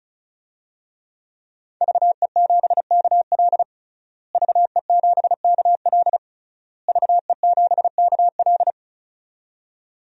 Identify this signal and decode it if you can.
Morse code, heard as a keyed tone, about 34 wpm, 700 Hz. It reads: VE7KL VE7KL VE7KL
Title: DXCC-100 Call Signs - 5 Characters - Repeated 3x - 35wpm